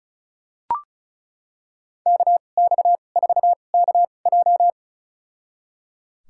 Morse code, heard as a keyed tone, about 36 wpm, spaced as for 25 wpm, 700 Hz. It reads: KX4KJ
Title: USA Call Signs - 2x Character Spacing - 35wpm